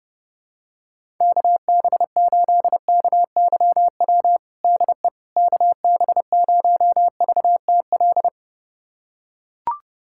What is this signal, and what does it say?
Morse code, heard as a keyed tone, about 30 wpm, 700 Hz. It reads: KB8KYW DE KB0VTL
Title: USA Callsign DE Callsign 30wpm